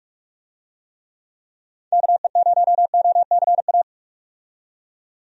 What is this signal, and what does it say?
Morse code, heard as a keyed tone, about 44 wpm, 700 Hz. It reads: KE0OCA
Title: USA Call Signs - 6 Characters - 45wpm